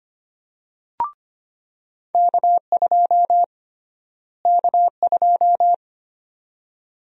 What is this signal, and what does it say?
Morse code, heard as a keyed tone, about 25 wpm, 700 Hz. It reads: K2 K2
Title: USA Call Sign Prefixes - Repeated 2x - 25wpm